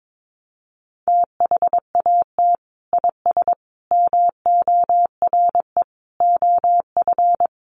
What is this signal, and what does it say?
Morse code, heard as a keyed tone, about 22 wpm, 700 Hz. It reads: THAT IS MORE OF
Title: Sentences from Top 1300 Words 22wpm